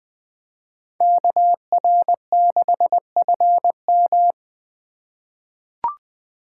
Morse code, heard as a keyed tone, about 20 wpm, 700 Hz. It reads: KR6FM
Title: USA Call Signs - 4x Character Spacing - 20wpm